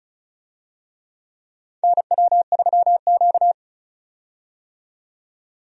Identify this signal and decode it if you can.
Morse code, heard as a keyed tone, about 34 wpm, 700 Hz. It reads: NW3Q
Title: USA Call Signs - 4 Characters - 35wpm